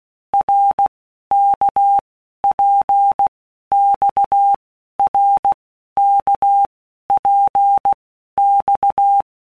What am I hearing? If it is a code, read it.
Morse code, heard as a keyed tone, about 16 wpm, 800 Hz. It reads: R K P X R K P X
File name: Basic Letter Groups By Blocks / Groups@15 / RKPX@15